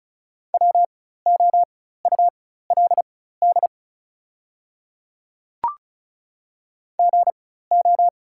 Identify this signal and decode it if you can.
Morse code, heard as a keyed tone, about 36 wpm, spaced as for 16 wpm, 690 Hz. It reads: WOULD GO
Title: 4x Character Spacing - Top 100 Words - 35wpm